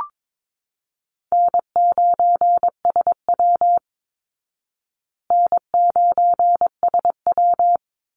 Morse code, heard as a keyed tone, about 22 wpm, 690 Hz. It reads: N9SW N9SW
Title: Easy Call Signs Repeated 2x - 22wpm